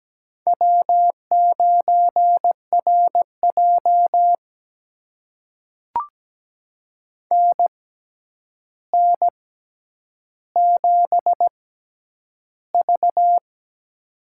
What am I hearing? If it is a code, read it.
Morse code, heard as a keyed tone, about 17 wpm, spaced as for 9 wpm, 700 Hz. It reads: W9RJ NN7V
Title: USA Call Signs - 4 Characters - 6x Character Spacing - 17wpm